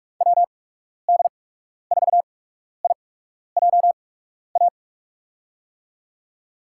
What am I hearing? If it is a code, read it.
Morse code, heard as a keyed tone, about 46 wpm, spaced as for 12 wpm, 700 Hz. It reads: WD4IJA
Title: USA Call Signs - 6 Characters - 8x Character Spacing - 45wpm